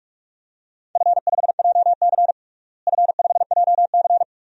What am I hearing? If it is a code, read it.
Morse code, heard as a keyed tone, about 46 wpm, 700 Hz. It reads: F5JC F5JC
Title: DXCC-100 Call Signs - 4 Characters - Repeated 3x - 45wpm